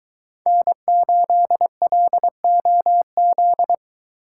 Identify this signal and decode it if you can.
Morse code, heard as a keyed tone, about 23 wpm, 700 Hz. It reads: N8LOZ